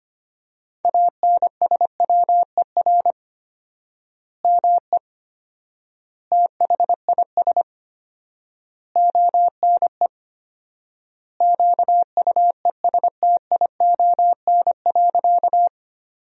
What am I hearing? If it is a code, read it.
Morse code, heard as a keyed tone, about 25 wpm, 700 Hz. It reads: ANSWER ME THIS ONE QUESTION.
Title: 4x Word-Spacing - Sentences from Top 400 Words - 25wpm